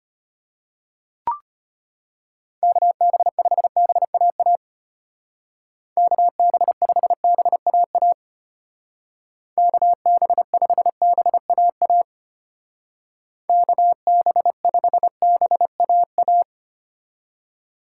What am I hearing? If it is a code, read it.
Morse code, heard as a keyed tone, about 32 wpm, 700 Hz. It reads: KB5BAA KB5BAA KB5BAA KB5BAA